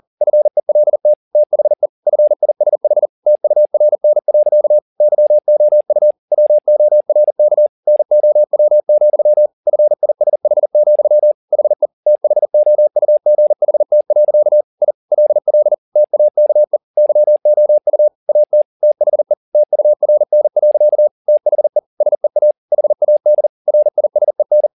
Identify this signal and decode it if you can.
Morse code, heard as a keyed tone, about 40 wpm, 600 Hz. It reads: FELT THE FISH TURN. YOU WORK NOW, FISH, HE THOUGHT. I LL TAKE YOU AT THE TURN. THE SEA HAD RISEN
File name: the old man and the sea 2 40wpm